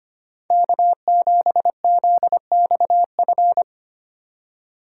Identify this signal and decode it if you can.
Morse code, heard as a keyed tone, about 28 wpm, 700 Hz. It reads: K7ZXF
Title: Speed Racing - USA Call Signs 25wpm